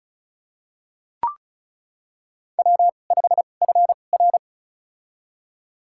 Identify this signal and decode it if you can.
Morse code, heard as a keyed tone, about 36 wpm, spaced as for 25 wpm, 700 Hz. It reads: W5FR
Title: USA Call Signs - 4 Characters - 2x Character Spacing - 35wpm